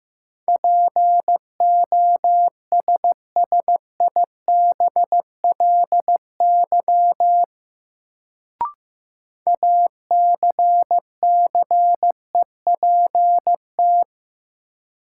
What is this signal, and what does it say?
Morse code, heard as a keyed tone, about 15 wpm, 700 Hz. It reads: POSSIBLY ACCEPT
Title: Sentences from Top 1300 Words - New Word Drill Alt - 15wpm